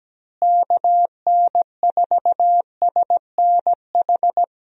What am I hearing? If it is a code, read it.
Morse code, heard as a keyed tone, about 17 wpm, 700 Hz. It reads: KN4SNH
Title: USA Call Signs - 6 Characters - Repeated 2x - 17wpm